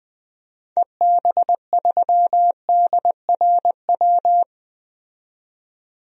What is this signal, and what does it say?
Morse code, heard as a keyed tone, about 20 wpm, 700 Hz. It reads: EB3DRW
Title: DXCC-100 Call Signs - 6 Characters - 20wpm